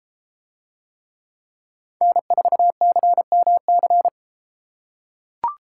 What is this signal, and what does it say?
Morse code, heard as a keyed tone, about 31 wpm, 700 Hz. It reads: N4CMC